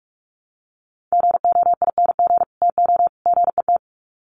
Morse code, heard as a keyed tone, about 44 wpm, 700 Hz. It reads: GOING TO GET
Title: Sets of 3 Words 45wpm